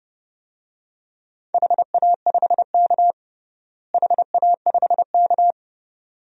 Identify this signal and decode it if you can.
Morse code, heard as a keyed tone, about 30 wpm, 700 Hz. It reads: HA5K HA5K
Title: DXCC-100 Call Signs - 4 Characters - Repeated 2x - 30wpm